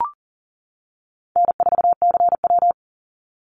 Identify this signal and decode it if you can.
Morse code, heard as a keyed tone, about 40 wpm, 700 Hz. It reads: N4CW
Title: Easy Call Signs - 40wpm